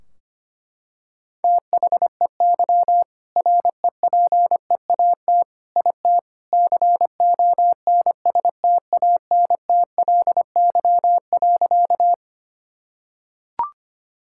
Morse code, heard as a keyed tone, about 25 wpm, 700 Hz. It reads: THEY REPEAT IT CONSTANTLY.
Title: Sentences from Top 1600 Words 25wpm